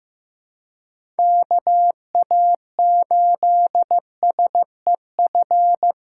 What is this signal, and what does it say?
Morse code, heard as a keyed tone, about 16 wpm, 700 Hz. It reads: KA8SEF
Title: USA Call Signs - 6 Characters - 3x Character Spacing - 15wpm